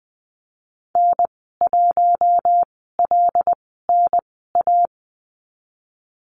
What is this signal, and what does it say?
Morse code, heard as a keyed tone, about 20 wpm, spaced as for 14 wpm, 700 Hz. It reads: N1LNA
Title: 076 Lesson 13g - Letter L - Call Signs 20wpm